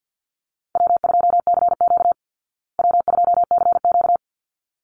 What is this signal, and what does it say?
Morse code, heard as a keyed tone, about 49 wpm, 700 Hz. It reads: W2CQ W2CQ
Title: Call Signs Repeated 3x - 50wpm